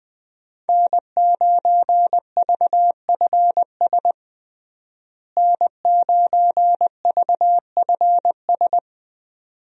Morse code, heard as a keyed tone, about 20 wpm, 700 Hz. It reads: N9VFS N9VFS
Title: USA Call Signs - 5 Characters - Repeated 2x - 20wpm